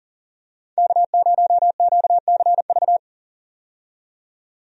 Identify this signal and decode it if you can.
Morse code, heard as a keyed tone, about 40 wpm, 700 Hz. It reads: K0QCV
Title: USA Call Signs - 5 Characters - 40wpm